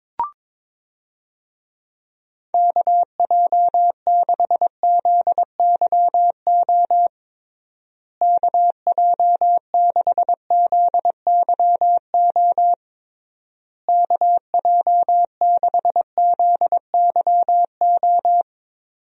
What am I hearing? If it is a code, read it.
Morse code, heard as a keyed tone, about 22 wpm, 700 Hz. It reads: KJ6ZYO KJ6ZYO KJ6ZYO
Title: USA Call Signs - 6 Characters - Repeated 3x - 22wpm